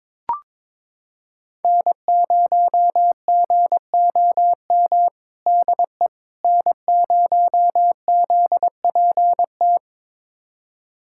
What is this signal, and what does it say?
Morse code, heard as a keyed tone, about 22 wpm, 700 Hz. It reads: N0GOM DE N0ZPT
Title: USA Callsign DE Callsign - 5 Characters - 22wpm